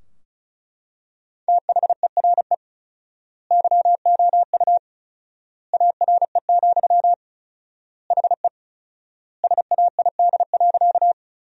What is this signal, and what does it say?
Morse code, heard as a keyed tone, about 34 wpm, 700 Hz. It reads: THERE YOU ARE, HE SAID.
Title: 4x Word-Spacing - Sentences from Top 100 Words - 35wpm